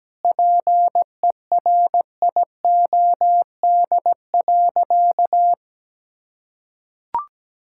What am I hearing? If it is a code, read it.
Morse code, heard as a keyed tone, about 17 wpm, 700 Hz. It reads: PERIOD.
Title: Sentences from Top 1000 Words - Review Part 5 - 17wpm